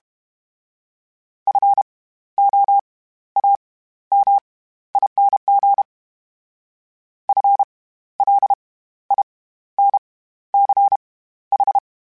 Morse code, heard as a keyed tone, about 32 wpm, spaced as for 14 wpm, 800 Hz. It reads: FOAMING FLINCH
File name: Special INGs / Special INGs-mp3s / SlowToFaster ing @30